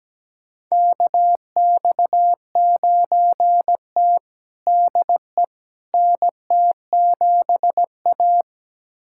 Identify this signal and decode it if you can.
Morse code, heard as a keyed tone, about 17 wpm, 700 Hz. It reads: KX9T DE NT7A
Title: USA Callsign DE Callsign - 4 Characters - 17wpm